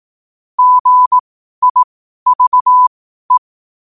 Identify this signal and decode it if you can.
Morse code, heard as a keyed tone, about 18 wpm, spaced as for 12 wpm, 1000 Hz. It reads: GIVE
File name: qotdfiles / 12wpm